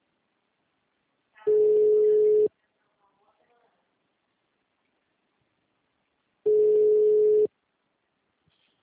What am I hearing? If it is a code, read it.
Ringback tone (Europe, 425 Hz), with speech.